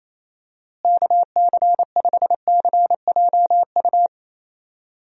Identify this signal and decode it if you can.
Morse code, heard as a keyed tone, about 28 wpm, 700 Hz. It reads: KC5CJU
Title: USA Call Signs - 6 Characters - Repeated 3x - 28wpm